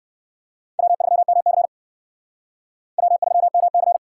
Morse code, heard as a keyed tone, about 55 wpm, 700 Hz. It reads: W3MC W3MC
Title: Speed Racing - USA 1x2 Call Signs 45wpm